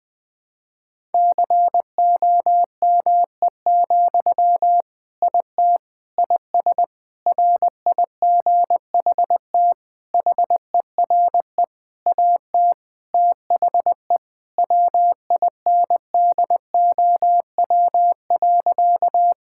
Morse code, heard as a keyed tone, about 21 wpm, 700 Hz. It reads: COME, IT IS RIGHT HERE AT THE WINDOW.
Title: Sentences from Top 1600 Words 20wpm